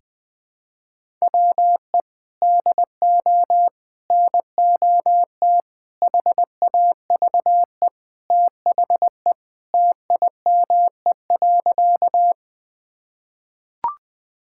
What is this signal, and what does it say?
Morse code, heard as a keyed tone, about 20 wpm, 700 Hz. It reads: WE DO NOT HAVE THE TIME.
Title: Sentences with 5-Word Phrases - 20wpm